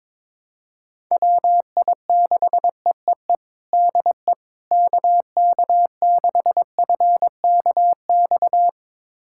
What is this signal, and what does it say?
Morse code, heard as a keyed tone, about 22 wpm, 700 Hz. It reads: WI6EEE DE KK6FKX
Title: USA Callsign DE Callsign - 6 Characters - 22wpm